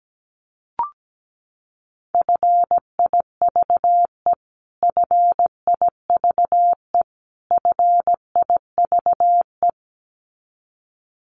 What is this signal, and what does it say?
Morse code, heard as a keyed tone, about 17 wpm, 700 Hz. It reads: FIVE FIVE FIVE
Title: Top 500 Words Repeated 17wpm